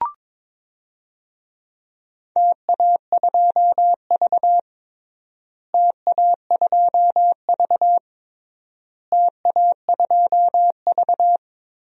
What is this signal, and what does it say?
Morse code, heard as a keyed tone, about 22 wpm, 700 Hz. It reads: TA2V TA2V TA2V
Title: DXCC-100 Call Signs - 4 Characters - Repeated 3x - 22wpm